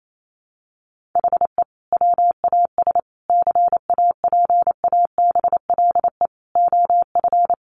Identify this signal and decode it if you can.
Morse code, heard as a keyed tone, about 28 wpm, 700 Hz. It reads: HE WAS CAPABLE OF
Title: Sentences from Top 1200 Words 28wpm